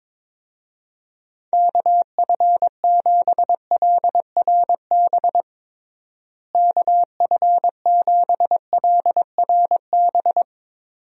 Morse code, heard as a keyed tone, about 21 wpm, 700 Hz. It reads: KF7LRB KF7LRB
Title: USA Call Signs - 6 Characters - Repeated 2x - 22wpm